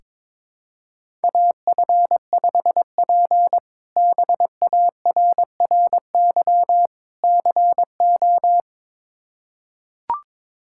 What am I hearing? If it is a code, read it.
Morse code, heard as a keyed tone, about 22 wpm, 700 Hz. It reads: AF5P BARRY CO
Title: K1USN SST QSO Practice - 22wpm